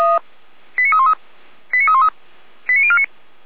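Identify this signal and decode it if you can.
Five-tone selective call (ZVEI), heard as quick runs of five tones, tones of about 70 ms. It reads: 89313 89313 89049
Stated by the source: ZVEI five-tone selective call received on an SDR, fire brigade Feuerwehr Orlamünde.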